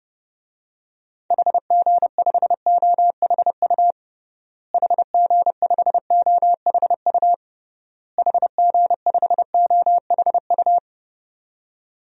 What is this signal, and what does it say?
Morse code, heard as a keyed tone, about 31 wpm, 700 Hz. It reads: HG5OHU HG5OHU HG5OHU
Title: DXCC-100 Call Signs - 6 Characters - Repeated 3x - 30wpm